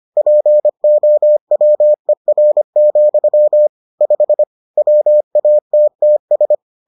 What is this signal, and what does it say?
Morse code, heard as a keyed tone, about 25 wpm, 600 Hz. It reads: POWER, 5 WATTS